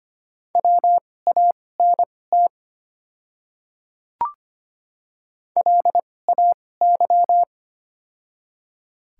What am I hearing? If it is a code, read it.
Morse code, heard as a keyed tone, about 25 wpm, spaced as for 18 wpm, 700 Hz. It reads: WANT LAY
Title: 2x Character Spacing - Top 500 Words - 25wpm